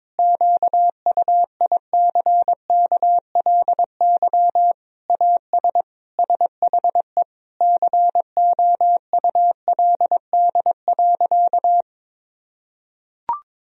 Morse code, heard as a keyed tone, about 22 wpm, 700 Hz. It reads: QUICKLY AS SHE COULD.
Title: Sentences from Top 1000 Words - Review Part 3 - 22wpm